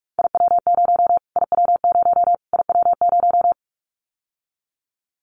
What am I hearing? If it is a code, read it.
Morse code, heard as a keyed tone, about 44 wpm, 700 Hz. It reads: IW0 IW0 IW0
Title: Callsign Prefixes for Top 25 DXCC - Repeated 3x - 45wpm